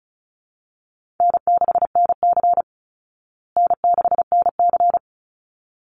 Morse code, heard as a keyed tone, about 34 wpm, 700 Hz. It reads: N6NC N6NC
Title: Easy Call Signs Repeated 2x - 35wpm